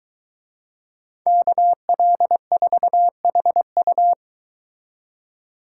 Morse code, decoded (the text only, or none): KL4HU